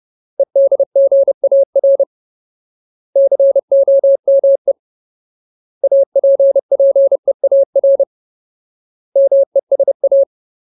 Morse code, heard as a keyed tone, about 30 wpm, 550 Hz. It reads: EDGAR COME APPEAR MESA
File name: Ultimate Word List / 30WPM Ultimate